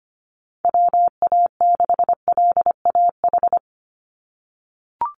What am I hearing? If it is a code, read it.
Morse code, heard as a keyed tone, about 25 wpm, 700 Hz. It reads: WA6LAH
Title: USA Call Signs 25wpm